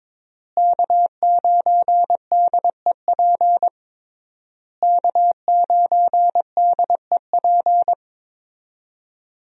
Morse code, heard as a keyed tone, about 22 wpm, 700 Hz. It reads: K9DEP K9DEP
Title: USA Call Signs - 5 Characters - Repeated 3x - 22wpm